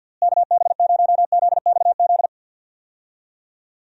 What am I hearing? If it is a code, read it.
Morse code, heard as a keyed tone, about 49 wpm, 700 Hz. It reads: KB0ZXZ
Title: USA Call Signs - 6 Characters - Repeated 3x - 50wpm